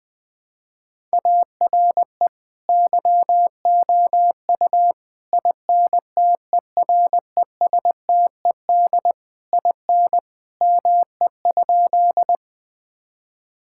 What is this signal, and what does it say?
Morse code, heard as a keyed tone, about 20 wpm, 700 Hz. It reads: ARE YOU INTERESTED IN ME?
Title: Sentences from Top 1000 Words - Review Part 6 - 20wpm